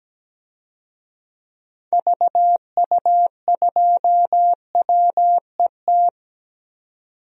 Morse code, heard as a keyed tone, about 17 wpm, 700 Hz. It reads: VU2WET
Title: DXCC-100 Call Signs - 6 Characters - 17wpm